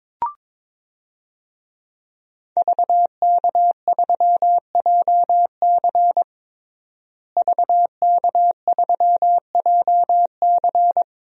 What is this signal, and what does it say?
Morse code, heard as a keyed tone, about 22 wpm, 700 Hz. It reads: VK3JC VK3JC
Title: DXCC-100 Call Signs - 5 Characters - Repeated 2x - 22wpm